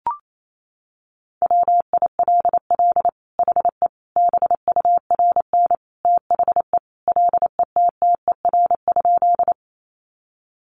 Morse code, heard as a keyed tone, about 28 wpm, 700 Hz. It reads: WILL HE BURN THE LETTER?
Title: Sentences from Top 1300 Words 28wpm